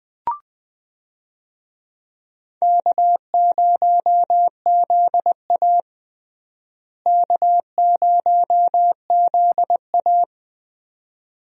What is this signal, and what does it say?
Morse code, heard as a keyed tone, about 20 wpm, 700 Hz. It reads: K0ZA K0ZA
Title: USA Call Signs - 4 Characters - Repeated 2x - 20wpm